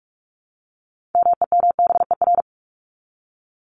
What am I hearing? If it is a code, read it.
Morse code, heard as a keyed tone, about 44 wpm, 700 Hz. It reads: MEMBER
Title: ES:S Suffix - Top 100 Words with Base Words 45wpm